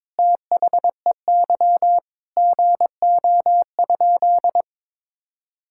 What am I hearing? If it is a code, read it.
Morse code, heard as a keyed tone, about 22 wpm, 700 Hz. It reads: THEY GO?
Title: Sentences from Top 1600 Words 22wpm